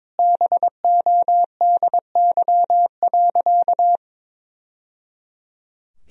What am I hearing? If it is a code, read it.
Morse code, heard as a keyed tone, about 22 wpm, 700 Hz. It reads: BODY.
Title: Sentences from Top 1000 Words - Review Part 2 - 22wpm